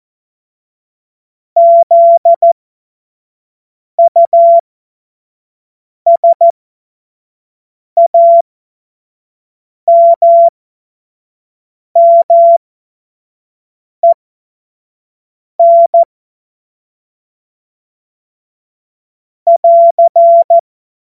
Morse code, heard as a keyed tone, about 14 wpm, spaced as for 5 wpm, 680 Hz. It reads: ZUSAMMEN <AR>
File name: Witz-08-Juli-2025-25-F70